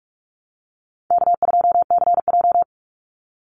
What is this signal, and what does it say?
Morse code, heard as a keyed tone, about 44 wpm, 700 Hz. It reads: K2CJ
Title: Call Signs - 45wpm